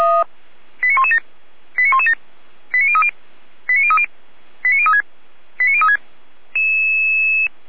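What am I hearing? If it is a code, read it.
Five-tone selective call (ZVEI), heard as quick runs of five tones, tones of about 70 ms. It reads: 89117 89117 89030 89030 89036 89036
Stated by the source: ZVEI five-tone selective call 89094 received on SDR.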